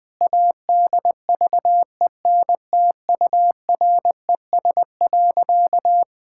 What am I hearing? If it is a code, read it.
Morse code, heard as a keyed tone, about 21 wpm, 700 Hz. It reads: ADVENTURES.
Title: Sentences with 7-Word Phrases - 20wpm